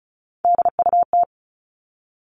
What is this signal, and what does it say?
Morse code, heard as a keyed tone, about 34 wpm, 700 Hz. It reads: DUT
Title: All Cut Numbers - 3-Digit 35wpm